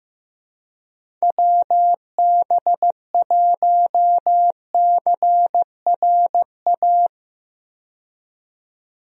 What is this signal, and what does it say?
Morse code, heard as a keyed tone, about 15 wpm, 700 Hz. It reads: WB1CRA
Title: USA Call Signs - 6 Characters - 15wpm